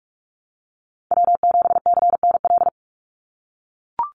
Morse code, heard as a keyed tone, about 44 wpm, 700 Hz. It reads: W7CNL
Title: NA SOTA Chaser Callsigns 45wpm